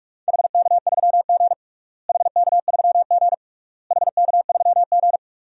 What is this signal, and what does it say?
Morse code, heard as a keyed tone, about 46 wpm, 700 Hz. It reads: HK3G HK3G HK3G
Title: DXCC-100 Call Signs - 4 Characters - Repeated 3x - 45wpm